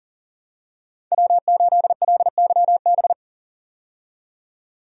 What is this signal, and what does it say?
Morse code, heard as a keyed tone, about 40 wpm, 700 Hz. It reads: W8LYB
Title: USA Call Signs - 5 Characters - 40wpm